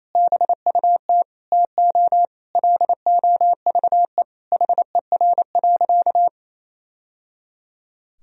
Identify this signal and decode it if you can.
Morse code, heard as a keyed tone, about 28 wpm, 700 Hz. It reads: BUT TO LOVE HER.
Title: Sentences from Top 1000 Words - Review Part 4 - 28wpm